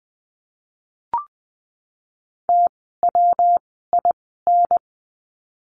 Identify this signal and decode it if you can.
Morse code, heard as a keyed tone, about 20 wpm, 700 Hz. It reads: T W I N
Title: 115 Lesson 19f - Letter w - Words 20wpm